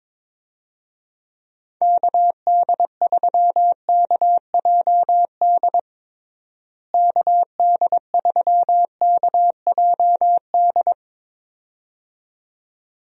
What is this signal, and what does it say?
Morse code, heard as a keyed tone, about 22 wpm, 700 Hz. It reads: KD3KJD KD3KJD
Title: USA Call Signs - 6 Characters - Repeated 2x - 22wpm